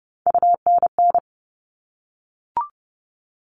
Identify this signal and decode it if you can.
Morse code, heard as a keyed tone, about 30 wpm, 700 Hz. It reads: UNN
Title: All Cut Numbers - 3-Digit 30wpm